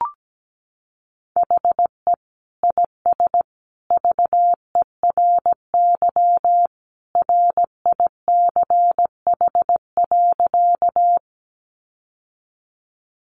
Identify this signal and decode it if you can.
Morse code, heard as a keyed tone, about 17 wpm, 700 Hz. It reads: HE IS VERY RICH.
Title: Sentences from Top 1100 Words 17wpm